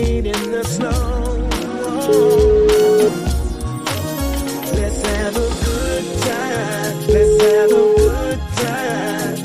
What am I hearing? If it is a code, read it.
Ringback tone (Europe, 425 Hz), with noise.